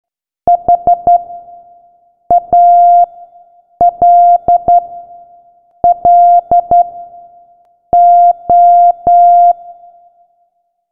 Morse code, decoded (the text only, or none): HALLO